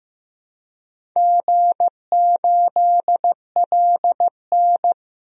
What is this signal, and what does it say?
Morse code, heard as a keyed tone, about 15 wpm, 700 Hz. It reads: G8LN
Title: DXCC-100 Call Signs - 4 Characters - Repeated 2x - 15wpm